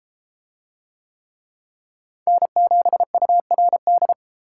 Morse code, heard as a keyed tone, about 30 wpm, 700 Hz. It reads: N7URD